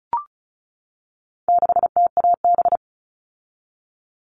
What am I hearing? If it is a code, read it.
Morse code, heard as a keyed tone, about 34 wpm, 700 Hz. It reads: 6TAB
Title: All Cut Numbers - 4-Digit 35wpm